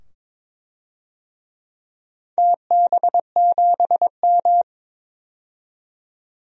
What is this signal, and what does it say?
Morse code, heard as a keyed tone, about 22 wpm, 700 Hz. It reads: TB7M
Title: DXCC-100 Call Signs - 4 Characters - 22wpm